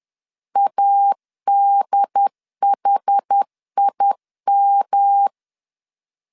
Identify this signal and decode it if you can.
Morse code, heard as a keyed tone, about 10 wpm, 800 Hz. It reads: ADHIM
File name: CW / DET / 1 3 1500C S B 10wpm 3335